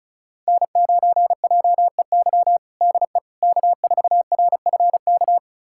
Morse code, heard as a keyed tone, about 36 wpm, 700 Hz. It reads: N9JEY DE K4RFK
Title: USA Callsign DE Callsign - 5 Characters - 35wpm